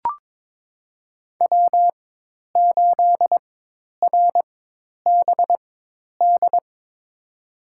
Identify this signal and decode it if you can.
Morse code, heard as a keyed tone, about 22 wpm, spaced as for 10 wpm, 700 Hz. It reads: W8RBD
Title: USA Call Signs - 5 Characters - 4x Character Spacing - 22wpm